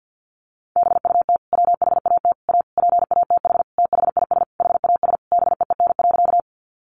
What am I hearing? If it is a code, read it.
Morse code, heard as a keyed tone, about 50 wpm, 700 Hz. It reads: BUT WHAT A PATH THIS HAS BEEN.
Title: Sentences from Top 1400 Words 50wpm